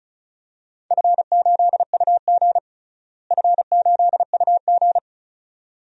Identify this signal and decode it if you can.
Morse code, heard as a keyed tone, about 36 wpm, 690 Hz. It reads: F8UG F8UG
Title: DXCC-100 Call Signs - 4 Characters - Repeated 2x - 35wpm